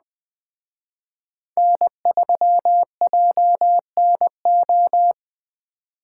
Morse code, heard as a keyed tone, about 20 wpm, 700 Hz. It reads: N3JNO